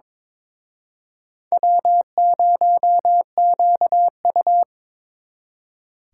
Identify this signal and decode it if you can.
Morse code, heard as a keyed tone, about 26 wpm, 700 Hz. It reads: W0QU
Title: Speed Racing - USA 1x2 Call Signs 22wpm